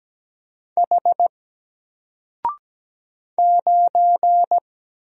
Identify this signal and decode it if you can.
Morse code, heard as a keyed tone, about 17 wpm, 700 Hz. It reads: H 9
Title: Single Letter-Number Alt 17wpm